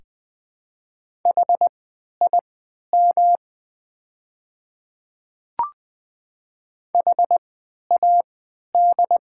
Morse code, heard as a keyed tone, about 20 wpm, spaced as for 11 wpm, 700 Hz. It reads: HIM HAD
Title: 3x Character Spacing - Top 100 Words - 20wpm